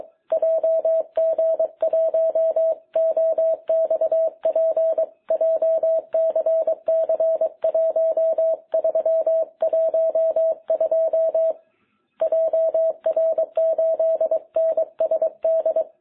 Morse code, heard as a keyed tone, about 22 wpm, 660 Hz. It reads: JG1OXP JCC1312 JR8NSD
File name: jh8cbh / 2021cw